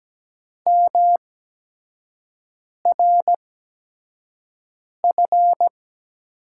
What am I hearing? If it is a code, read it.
Morse code, heard as a keyed tone, about 17 wpm, spaced as for 5 wpm, 700 Hz. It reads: MRF
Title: USA Call Signs - 6 Characters - 8x Character Spacing - 17wpm